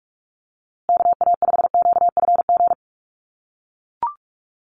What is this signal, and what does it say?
Morse code, heard as a keyed tone, about 44 wpm, 700 Hz. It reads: KA5QFG
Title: USA Call Signs Repeated 3x 45wpm